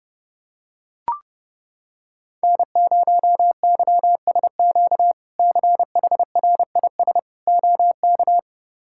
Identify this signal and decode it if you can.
Morse code, heard as a keyed tone, about 30 wpm, 700 Hz. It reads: N0YSQ CHRIS OK
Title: K1USN SST QSO Practice - 30wpm